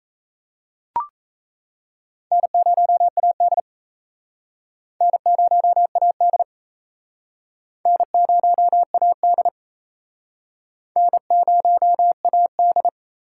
Morse code, heard as a keyed tone, about 35 wpm, 700 Hz. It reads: N0AD N0AD N0AD N0AD